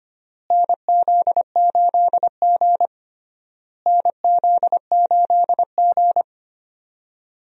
Morse code, heard as a keyed tone, about 25 wpm, 700 Hz. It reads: NZ8G NZ8G
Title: USA Call Signs - 4 Characters - Repeated 3x - 25wpm